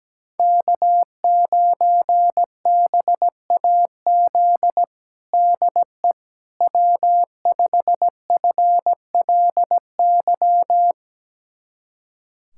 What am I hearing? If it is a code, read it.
Morse code, heard as a keyed tone, about 17 wpm, 700 Hz. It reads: K9BAZ DE W5FLY
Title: USA Callsign DE Callsign - 5 Characters - 17wpm